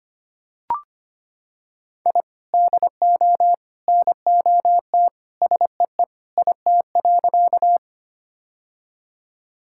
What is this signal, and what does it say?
Morse code, heard as a keyed tone, about 25 wpm, 700 Hz. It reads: I DO NOT SEE IT.
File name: practice / 25wpm / Sentences with 3-Word Phrases - 25wpm